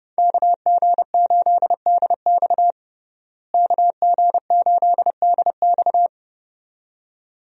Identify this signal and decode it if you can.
Morse code, heard as a keyed tone, about 30 wpm, 700 Hz. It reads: KG8DX KG8DX
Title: USA Call Signs - 5 Characters - Repeated 2x - 30wpm